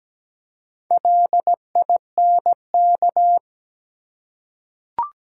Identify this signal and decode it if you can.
Morse code, heard as a keyed tone, about 17 wpm, 700 Hz. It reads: LINK
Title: Four Letter Words - 17wpm